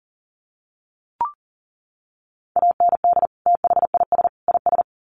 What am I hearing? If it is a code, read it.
Morse code, heard as a keyed tone, about 40 wpm, 700 Hz. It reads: AND THIS IS
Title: Sets of 3 Words 40wpm